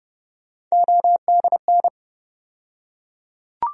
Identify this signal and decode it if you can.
Morse code, heard as a keyed tone, about 30 wpm, 700 Hz. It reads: ODN
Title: USA Call Sign Postfixes - 3 Letters 30wpm